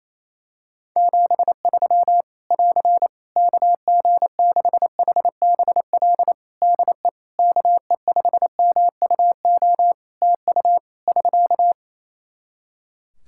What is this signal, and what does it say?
Morse code, heard as a keyed tone, about 28 wpm, 700 Hz. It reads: 73 <AR> KG6HBL DE KE5MUO TU <SK>
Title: Practice QSO - Exchange 4 - Ending - 28wpm